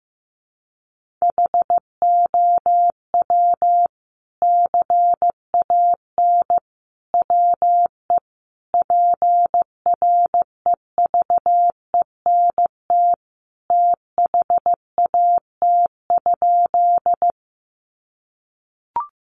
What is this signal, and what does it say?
Morse code, heard as a keyed tone, about 15 wpm, 700 Hz. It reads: HOW CAN WE PREVENT THAT?
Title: Sentences from Top 1100 Words 15wpm